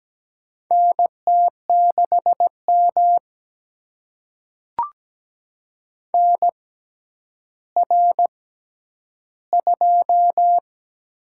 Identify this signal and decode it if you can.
Morse code, heard as a keyed tone, about 17 wpm, spaced as for 10 wpm, 700 Hz. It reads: NT6M NR2
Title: USA Call Signs - 4 Characters - 6x Character Spacing - 17wpm